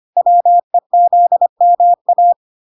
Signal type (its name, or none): Morse code